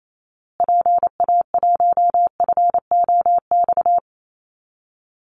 Morse code, heard as a keyed tone, about 28 wpm, 700 Hz. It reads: PA1FOX
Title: Call Signs Repeated 2x - 28wpm